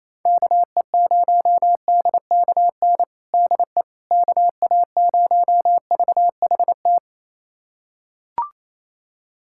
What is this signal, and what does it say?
Morse code, heard as a keyed tone, about 28 wpm, 700 Hz. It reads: KE0DKN DE KA0VHT
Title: USA Callsign DE Callsign - 6 Characters - 28wpm